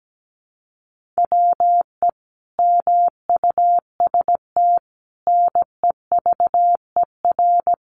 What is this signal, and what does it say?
Morse code, heard as a keyed tone, about 17 wpm, 700 Hz. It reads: WE MUST NEVER
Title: Sentences from Top 1500 Words 17wpm